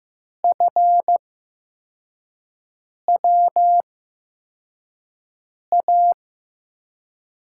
Morse code, heard as a keyed tone, about 15 wpm, spaced as for 4 wpm, 700 Hz. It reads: FWA
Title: USA Call Signs - 6 Characters - 8x Character Spacing - 15wpm